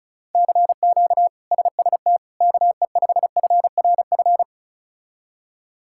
Morse code, heard as a keyed tone, about 36 wpm, 700 Hz. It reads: CQ SST KE5FRF
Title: K1USN SST QSO Practice - 35wpm